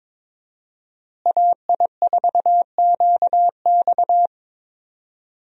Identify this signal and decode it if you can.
Morse code, heard as a keyed tone, about 22 wpm, 700 Hz. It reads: AI4QX
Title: USA Call Signs - 5 Characters - Repeated 2x - 22wpm